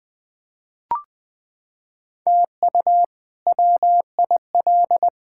Morse code, heard as a keyed tone, about 20 wpm, 700 Hz. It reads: TU WIL
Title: K1USN SST QSO Practice - 20wpm